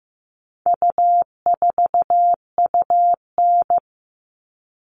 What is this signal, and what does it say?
Morse code, heard as a keyed tone, about 15 wpm, 700 Hz. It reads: U4UN
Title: All Cut Numbers - 4-Digit 15wpm